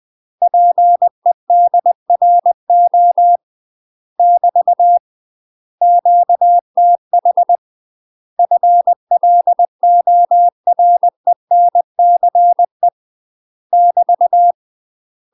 Morse code, heard as a keyed tone, about 20 wpm, 700 Hz. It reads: PEDRO <BT> QTH FLORENCE <BT>
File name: UK MASTER QSOs TRAINING / 20WPM / UK MASTER QSOs 0042